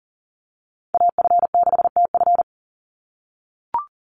Morse code, heard as a keyed tone, about 40 wpm, 700 Hz. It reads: AF6TF
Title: USA Call Signs 40wpm